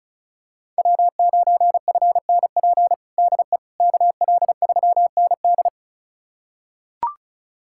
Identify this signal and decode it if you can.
Morse code, heard as a keyed tone, about 34 wpm, 700 Hz. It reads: W9FNP DE KL3ND
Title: USA Callsign DE Callsign - 5 Characters - 35wpm